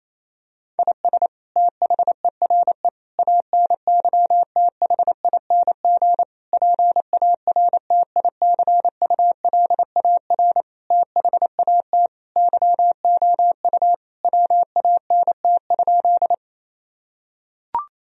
Morse code, heard as a keyed tone, about 28 wpm, 700 Hz. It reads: IS THERE ANYTHING PARTICULAR THAT YOU WANT?
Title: Sentences from Top 1000 Words - Review Part 4 - 28wpm